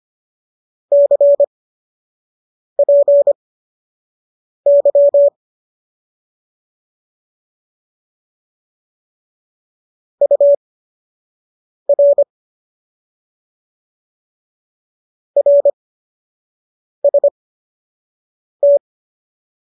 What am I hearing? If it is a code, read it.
Morse code, heard as a keyed tone, about 25 wpm, spaced as for 6 wpm, 570 Hz. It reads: CPY UR RST